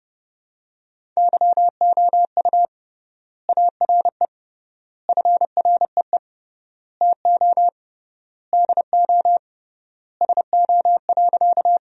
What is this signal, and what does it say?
Morse code, heard as a keyed tone, about 30 wpm, 700 Hz. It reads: YOU ARE FREE TO DO SO.
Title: 3x Word-Spacing - Sentences from Top 500 Words - 30wpm